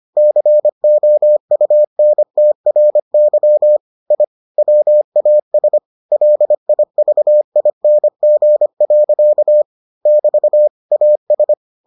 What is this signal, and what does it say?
Morse code, heard as a keyed tone, about 25 wpm, 600 Hz. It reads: COUNTRY I WAS LIVING. <BT> AS